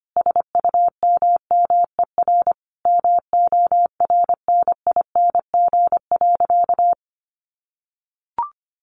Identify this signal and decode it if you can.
Morse code, heard as a keyed tone, about 25 wpm, 690 Hz. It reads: SUMMER MORNING.
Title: Sentences from Top 1100 Words 25wpm